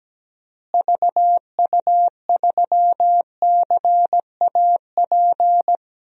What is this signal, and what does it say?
Morse code, heard as a keyed tone, about 17 wpm, 700 Hz. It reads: VU3CAP
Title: DXCC-100 Call Signs - 6 Characters - Repeated 3x - 17wpm